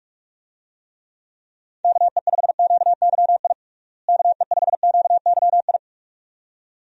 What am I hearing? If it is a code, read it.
Morse code, heard as a keyed tone, about 42 wpm, 700 Hz. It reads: KE5QYI KE5QYI
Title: USA Call Signs - 6 Characters - Repeated 2x - 45wpm